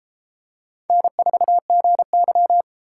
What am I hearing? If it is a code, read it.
Morse code, heard as a keyed tone, about 31 wpm, 700 Hz. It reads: N4GY